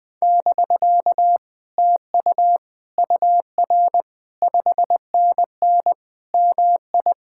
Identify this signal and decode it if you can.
Morse code, heard as a keyed tone, about 20 wpm, 700 Hz. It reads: <BK> TU UR 5NN MI
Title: Speed-Racing Common QSO Phrases - 17wpm